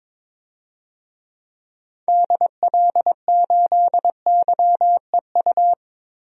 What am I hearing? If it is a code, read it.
Morse code, heard as a keyed tone, about 23 wpm, 700 Hz. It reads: DL8YEU
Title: DXCC-100 Call Signs - 6 Characters - Repeated 2x - 22wpm